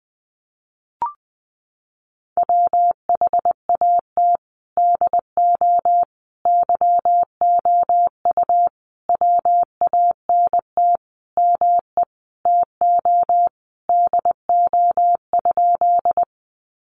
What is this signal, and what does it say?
Morse code, heard as a keyed tone, about 20 wpm, 700 Hz. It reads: WHAT DO YOU WANT ME TO DO?
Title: Sentences from Top 200 Words 20wpm